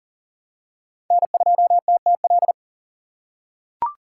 Morse code, heard as a keyed tone, about 40 wpm, 700 Hz. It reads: N2TTL
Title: USA Call Signs - 5 Characters - 6x Character Spacing - 40wpm